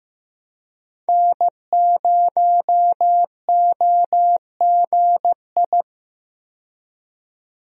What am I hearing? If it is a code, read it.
Morse code, heard as a keyed tone, about 16 wpm, 700 Hz. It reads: N0OGI
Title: USA Call Signs - 5 Characters - 15wpm